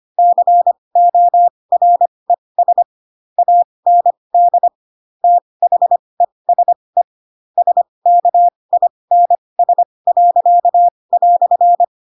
Morse code, heard as a keyed tone, about 25 wpm, spaced as for 20 wpm, 700 Hz. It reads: CORES AND THESE SKINS."
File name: Pinocchio / Pinocchio ch7 15 25 ik1hge